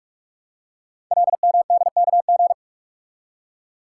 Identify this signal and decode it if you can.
Morse code, heard as a keyed tone, about 44 wpm, 700 Hz. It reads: LMDKG
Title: Code Groups - 5 Letters - 45wpm